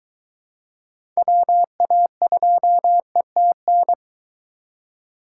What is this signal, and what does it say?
Morse code, heard as a keyed tone, about 22 wpm, 700 Hz. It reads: WA2ETN